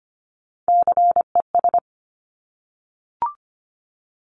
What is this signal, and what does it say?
Morse code, heard as a keyed tone, about 25 wpm, 700 Hz. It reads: CES
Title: Top Three Letter Combinations in Words 25wpm